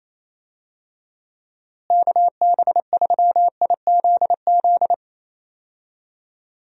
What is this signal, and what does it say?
Morse code, heard as a keyed tone, about 28 wpm, 700 Hz. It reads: KB3IZZ
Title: USA Call Signs - 6 Characters - Repeated 2x - 28wpm